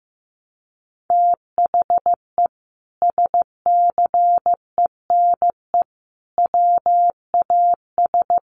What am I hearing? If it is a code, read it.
Morse code, heard as a keyed tone, about 15 wpm, 700 Hz. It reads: THE SCENE WAS
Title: Sentences from Top 1100 Words 15wpm